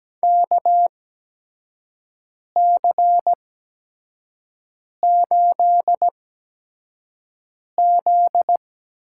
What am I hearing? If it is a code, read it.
Morse code, heard as a keyed tone, about 17 wpm, spaced as for 5 wpm, 700 Hz. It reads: KC8Z
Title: USA Call Signs - 6 Characters - 8x Character Spacing - 17wpm